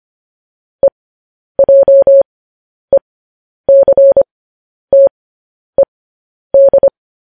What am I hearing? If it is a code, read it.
Morse code, heard as a keyed tone, about 25 wpm, spaced as for 10 wpm, 550 Hz. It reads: EJECTED
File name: podcast / News headlines-1x-25x10